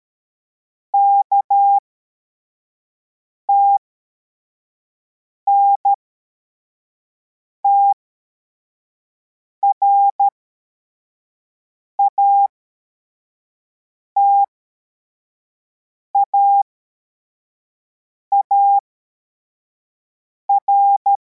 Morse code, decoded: KTNTR ATAAR